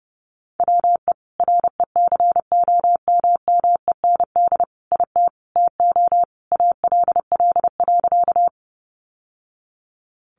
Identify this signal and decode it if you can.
Morse code, heard as a keyed tone, about 30 wpm, 700 Hz. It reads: WE RECOMMEND IT TO ALL.
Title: Sentences from Top 1200 Words 30wpm